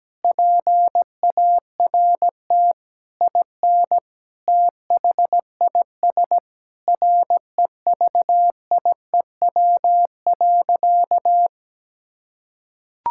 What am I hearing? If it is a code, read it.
Morse code, heard as a keyed tone, about 17 wpm, 700 Hz. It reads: PART IN THIS REVIEW.
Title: Sentences from Top 1000 Words - Review Part 6 - 17wpm